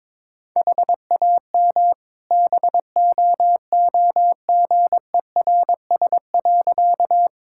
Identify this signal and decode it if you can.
Morse code, heard as a keyed tone, about 22 wpm, 700 Hz. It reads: HAM BOOGERS.
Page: morse-code-ninja-practice-22wpm